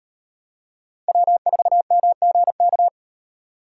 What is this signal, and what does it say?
Morse code, heard as a keyed tone, about 38 wpm, 700 Hz. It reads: W4MGK